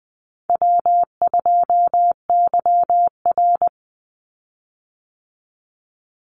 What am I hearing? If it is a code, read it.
Morse code, heard as a keyed tone, about 20 wpm, 700 Hz. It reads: W2YR
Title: Easy Call Signs Repeated 3x - 20wpm